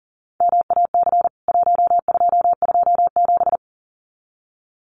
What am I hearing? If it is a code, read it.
Morse code, heard as a keyed tone, about 37 wpm, 700 Hz. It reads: MAC 1227
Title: CWT Exchange 40wpm